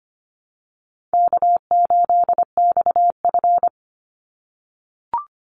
Morse code, heard as a keyed tone, about 25 wpm, 700 Hz. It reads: K8XF
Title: Call Signs - 25wpm